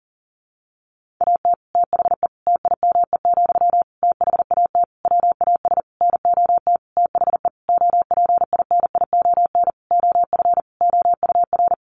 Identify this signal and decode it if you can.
Morse code, heard as a keyed tone, about 40 wpm, 700 Hz. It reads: AT THE TIME, THAT WAS NOT THE OPINION OF OUR
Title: Sentences from Top 1400 Words 40wpm